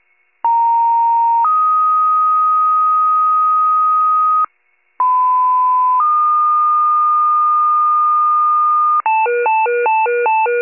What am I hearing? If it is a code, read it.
Two-tone page, heard as a one-second tone, then a three-second tone, about 930 then 1250 Hz, about 1010 then 1220 Hz. It is heard from 0.4 to 9.0 s.
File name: wcalert